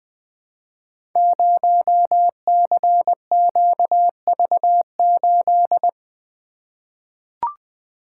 Morse code, heard as a keyed tone, about 20 wpm, 700 Hz. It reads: 0CQV8
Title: Code Groups - 5 Letter:Numbers - 20wpm